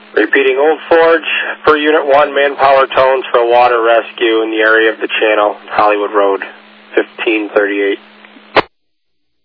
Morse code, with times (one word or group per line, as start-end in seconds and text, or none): none